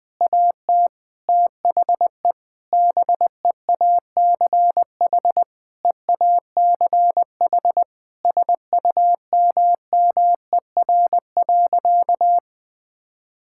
Morse code, heard as a keyed tone, about 20 wpm, 700 Hz. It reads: AT THE BEACH EACH SUMMER.
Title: Difficult Sentences with Hints - 20wpm